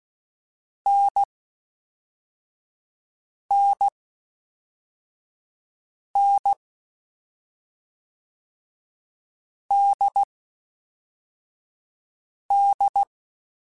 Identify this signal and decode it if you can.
Morse code, heard as a keyed tone, about 16 wpm, 800 Hz. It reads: N N N D D
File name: Difficult Combos / Difficult Combos-mp3s / NDB NDB6 @5